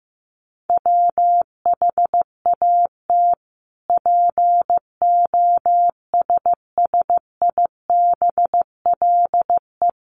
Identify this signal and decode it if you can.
Morse code, heard as a keyed tone, about 14 wpm, 700 Hz. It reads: WHAT POSSIBLE
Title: Sentences from Top 1300 Words 15wpm